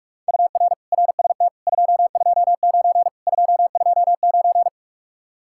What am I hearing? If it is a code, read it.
Morse code, heard as a keyed tone, about 49 wpm, 700 Hz. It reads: UR RST 229 229